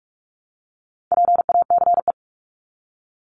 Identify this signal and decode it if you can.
Morse code, heard as a keyed tone, about 44 wpm, 700 Hz. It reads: PACE
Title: Top 1751-2000 Words 45wpm